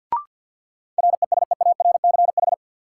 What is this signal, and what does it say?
Morse code, heard as a keyed tone, about 53 wpm, 700 Hz. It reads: RESEARCH
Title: Sentences from Top 1000 Words - Review Part 4 - New Word Drill - 50wpm